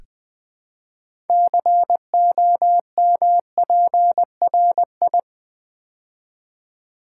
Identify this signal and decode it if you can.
Morse code, heard as a keyed tone, about 20 wpm, 700 Hz. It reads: COMPRI
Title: Build A Word - 3 Letters At A Time - 8-Letter Words - 20wpm